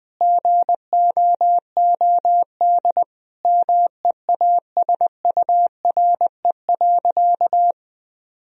Morse code, heard as a keyed tone, about 21 wpm, 700 Hz. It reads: GOOD MEASURE.
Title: Sentences from Top 1000 Words - Review Part 2 - 20wpm